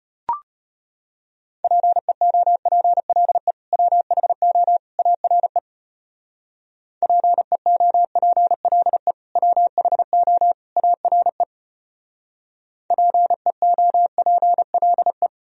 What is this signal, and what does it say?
Morse code, heard as a keyed tone, about 33 wpm, 700 Hz. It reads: PEOPLE WHO ARE PEOPLE WHO ARE PEOPLE